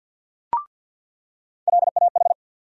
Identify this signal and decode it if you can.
Morse code, heard as a keyed tone, about 48 wpm, 700 Hz. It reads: LAH
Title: USA Call Sign Postfixes - 3 Letters 50wpm